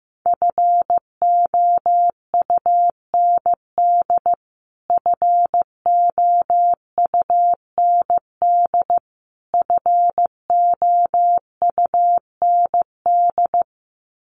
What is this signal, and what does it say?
Morse code, heard as a keyed tone, about 15 wpm, 700 Hz. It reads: FOUND FOUND FOUND
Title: Top 500 Words Repeated 15wpm